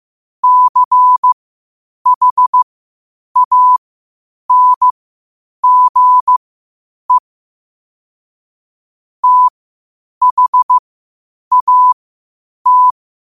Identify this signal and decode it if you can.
Morse code, heard as a keyed tone, about 15 wpm, spaced as for 8 wpm, 1000 Hz. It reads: CHANGE THAT
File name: qotdfiles / 8wpm